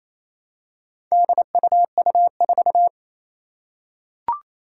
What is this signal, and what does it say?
Morse code, heard as a keyed tone, about 27 wpm, 700 Hz. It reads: DUU4
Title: Four Challenging Letter-Number Combinations 28wpm